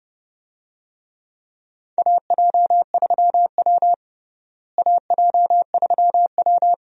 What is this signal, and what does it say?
Morse code, heard as a keyed tone, about 30 wpm, 700 Hz. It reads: AJ3W AJ3W
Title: USA Call Signs - 4 Characters - Repeated 2x - 30wpm